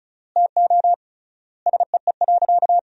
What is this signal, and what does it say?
Morse code, heard as a keyed tone, about 34 wpm, 700 Hz. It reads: TO SEE.
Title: 3x Word-Spacing - Sentences from Top 200 Words - 35wpm